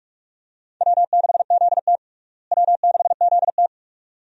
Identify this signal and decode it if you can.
Morse code, heard as a keyed tone, about 41 wpm, 700 Hz. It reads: W6ZT W6ZT
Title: USA Call Signs - 4 Characters - Repeated 2x - 45wpm